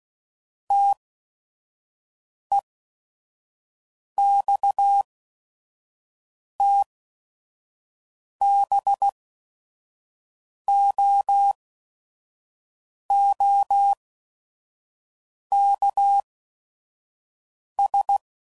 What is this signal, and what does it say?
Morse code, heard as a keyed tone, about 16 wpm, spaced as for 5 wpm, 800 Hz. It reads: TEXTBOOKS
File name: CW Code Groups @5WPM / Technical Subjects / Gates Intro